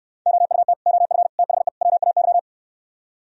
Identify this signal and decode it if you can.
Morse code, heard as a keyed tone, about 74 wpm, 700 Hz. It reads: OUT OF THE WAY